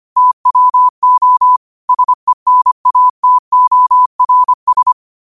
Morse code, heard as a keyed tone, about 25 wpm, 1000 Hz. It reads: TWO SENATORS
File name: qotdfiles / 25wpm